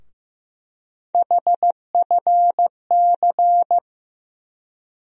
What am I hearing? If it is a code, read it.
Morse code, heard as a keyed tone, about 15 wpm, 700 Hz. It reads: HFC
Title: USA Call Sign Postfixes - 3 Letters 15wpm